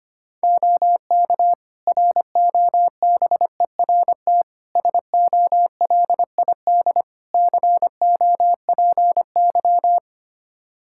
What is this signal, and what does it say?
Morse code, heard as a keyed tone, about 25 wpm, 700 Hz. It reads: OK ROBERT SOLID COPY
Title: K1USN SST QSO Practice 25wpm